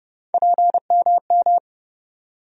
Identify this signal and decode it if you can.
Morse code, heard as a keyed tone, about 30 wpm, 700 Hz. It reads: PMM
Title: USA Call Sign Postfixes - 3 Letters 30wpm